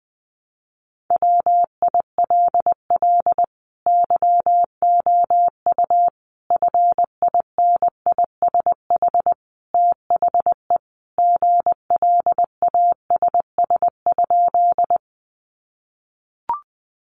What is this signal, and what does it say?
Morse code, heard as a keyed tone, about 20 wpm, 700 Hz. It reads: WILL YOU FINISH THE GLASS?
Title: Sentences from Top 1300 Words 20wpm